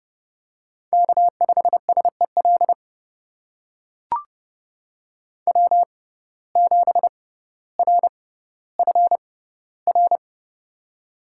Morse code, heard as a keyed tone, about 30 wpm, spaced as for 15 wpm, 700 Hz. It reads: K5SEL W7RFR
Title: USA Call Signs - 5 Characters - 6x Character Spacing - 30wpm